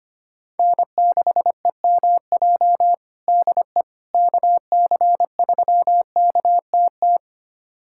Morse code, heard as a keyed tone, about 25 wpm, 700 Hz. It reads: N6EMJ DE KC3KTT
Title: USA Callsign DE Callsign 25wpm